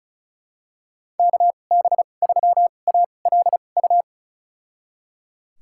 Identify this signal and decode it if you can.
Morse code, heard as a keyed tone, about 34 wpm, spaced as for 25 wpm, 700 Hz. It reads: KB3ALU
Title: USA Call Signs - 6 Characters - 2x Character Spacing - 35wpm